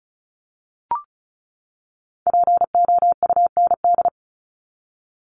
Morse code, heard as a keyed tone, about 34 wpm, 700 Hz. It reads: POUND
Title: Top 500 Words 35wpm